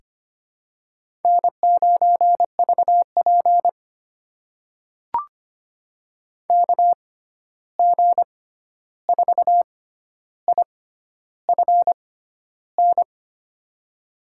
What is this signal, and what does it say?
Morse code, heard as a keyed tone, about 25 wpm, spaced as for 11 wpm, 700 Hz. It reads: N9VP KG4IFN
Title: USA Call Signs - 6x Character Spacing - 25wpm